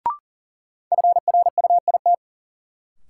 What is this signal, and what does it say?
Morse code, heard as a keyed tone, about 40 wpm, 700 Hz. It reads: FRUIT
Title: Sentences from Top 1300 Words - New Word Drill - 40wpm